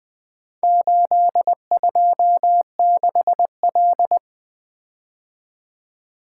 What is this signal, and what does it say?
Morse code, heard as a keyed tone, about 20 wpm, 700 Hz. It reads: 826L
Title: Four Letter-Number Combinations - 20wpm